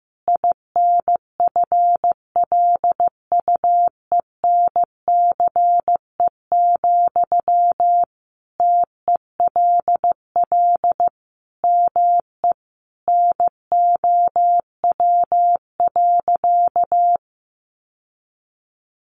Morse code, heard as a keyed tone, about 15 wpm, 700 Hz. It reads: INFLUENCE, TELL ME NOW.
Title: Sentences from Top 1100 Words 15wpm